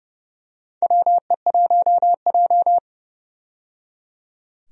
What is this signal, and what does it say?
Morse code, heard as a keyed tone, about 30 wpm, 700 Hz. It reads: WE1J
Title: USA Call Signs - 4 Characters - 30wpm